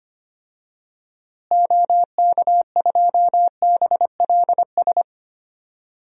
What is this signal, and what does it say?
Morse code, heard as a keyed tone, about 25 wpm, 700 Hz. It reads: OK2BLS
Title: DXCC-100 Call Signs - 6 Characters - Repeated 2x - 25wpm